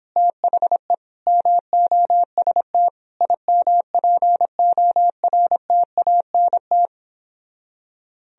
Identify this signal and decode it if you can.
Morse code, heard as a keyed tone, about 29 wpm, 700 Hz. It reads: THE MOST IMPORTANT